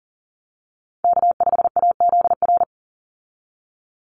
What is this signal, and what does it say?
Morse code, heard as a keyed tone, about 40 wpm, 700 Hz. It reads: K5AZR
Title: Call Signs Repeated 2x - 40wpm